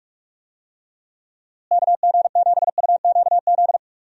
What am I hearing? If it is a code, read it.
Morse code, heard as a keyed tone, about 44 wpm, 700 Hz. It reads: KG7UQZ
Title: USA Call Signs - 6 Characters - Repeated 2x - 45wpm